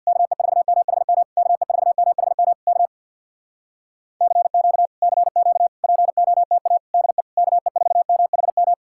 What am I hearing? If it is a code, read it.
Morse code, heard as a keyed tone, about 62 wpm, 700 Hz. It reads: KE4MHM KE4MHM K CQ CQ POTA DE KE4MHM
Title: Speed-Racing Common QSO Phrases - 50wpm